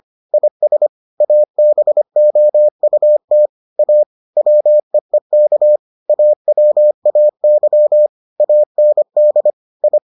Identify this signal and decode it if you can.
Morse code, heard as a keyed tone, about 25 wpm, 600 Hz. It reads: IS ABOUT A WEEK AWAY AND I